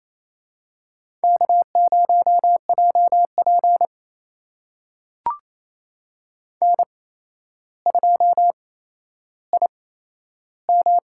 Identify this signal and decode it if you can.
Morse code, heard as a keyed tone, about 28 wpm, spaced as for 12 wpm, 700 Hz. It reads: K0JP N2IM
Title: USA Call Signs - 8x Character Spacing - 28wpm